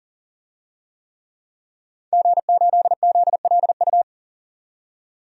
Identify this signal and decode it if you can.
Morse code, heard as a keyed tone, about 37 wpm, 700 Hz. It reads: G8ZLU
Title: DXCC-100 Call Signs - 5 Characters - 40wpm